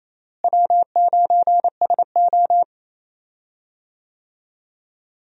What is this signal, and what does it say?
Morse code, heard as a keyed tone, about 27 wpm, 700 Hz. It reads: W9SO
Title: USA Call Signs - 4 Characters - Repeated 3x - 28wpm